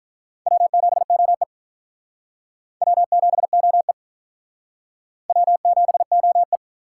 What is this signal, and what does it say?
Morse code, heard as a keyed tone, about 46 wpm, 700 Hz. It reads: W7OE W7OE W7OE